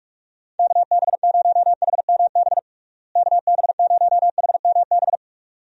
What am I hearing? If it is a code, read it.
Morse code, heard as a keyed tone, about 42 wpm, 700 Hz. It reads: KB0HMB KB0HMB
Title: USA Call Signs - 6 Characters - Repeated 2x - 45wpm